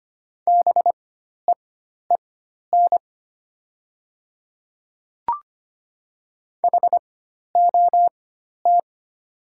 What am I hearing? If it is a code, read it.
Morse code, heard as a keyed tone, about 25 wpm, spaced as for 12 wpm, 700 Hz. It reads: BEEN HOT
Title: 4x Character Spacing - Top 100 Words - 25wpm